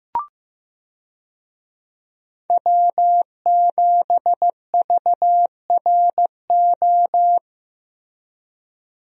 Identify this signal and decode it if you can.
Morse code, heard as a keyed tone, about 15 wpm, 700 Hz. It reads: W7VRO
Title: USA Call Signs - 5 Characters - Repeated 3x - 15wpm